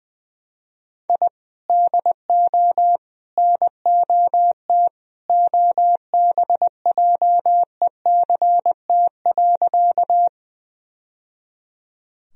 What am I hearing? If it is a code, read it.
Morse code, heard as a keyed tone, about 20 wpm, 700 Hz. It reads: I DO NOT OBJECT.
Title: Sentences from Top 1000 Words - Review Part 3 - 20wpm